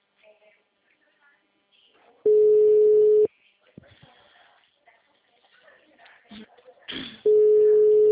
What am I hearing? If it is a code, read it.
Ringback tone (Europe, 425 Hz), with speech.
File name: out- -2021- - - .11